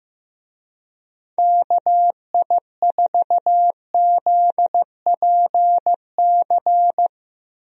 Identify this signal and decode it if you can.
Morse code, heard as a keyed tone, about 15 wpm, 700 Hz. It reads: KI4ZPC
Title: USA Call Signs - 6 Characters - 15wpm